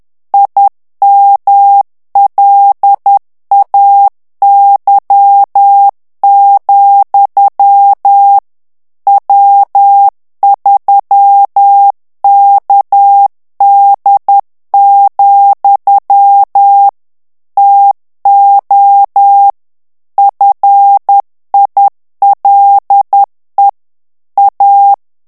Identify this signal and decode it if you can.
Morse code, heard as a keyed tone, about 10 wpm, 800 Hz. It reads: IMLAY, W3KD, TO FILE A
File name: ANGLAIS texte clair 3 vitesse 10 mots minute